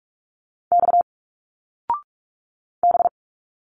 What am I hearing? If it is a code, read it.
Morse code, heard as a keyed tone, about 44 wpm, 700 Hz. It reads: X B
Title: Single Letters 45wpm